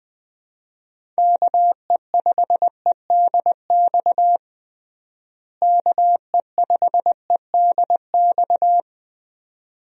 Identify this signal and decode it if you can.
Morse code, heard as a keyed tone, about 21 wpm, 700 Hz. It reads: KE5EDX KE5EDX
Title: USA Call Signs - 6 Characters - Repeated 2x - 20wpm